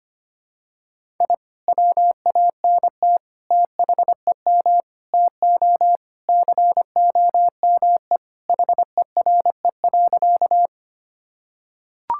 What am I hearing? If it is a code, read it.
Morse code, heard as a keyed tone, about 25 wpm, 700 Hz. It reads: I WANT THEM TO COME HERE.
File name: practice / 25wpm / Sentences with 2-Word Phrases - 25wpm